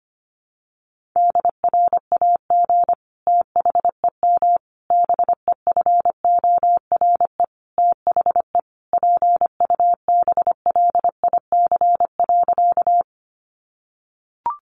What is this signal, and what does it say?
Morse code, heard as a keyed tone, about 25 wpm, 700 Hz. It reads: DRAG THEM BEFORE THE PUBLIC.
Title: Sentences from Top 1500 Words 25wpm